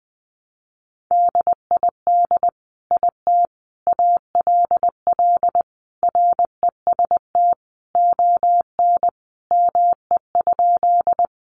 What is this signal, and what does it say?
Morse code, heard as a keyed tone, about 19 wpm, 700 Hz. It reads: DID IT ALL REST ON ME?
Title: Sentences from Top 500 Words 20wpm